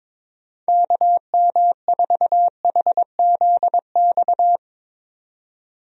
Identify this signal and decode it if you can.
Morse code, heard as a keyed tone, about 21 wpm, 700 Hz. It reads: KM4HZX
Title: USA Call Signs - 6 Characters - 22wpm